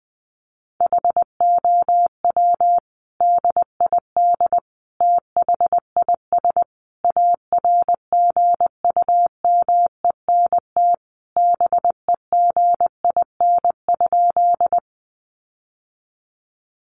Morse code, heard as a keyed tone, about 20 wpm, 700 Hz. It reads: HOW DID THIS ARGUMENT BEGIN?
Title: Sentences from Top 1400 Words 20wpm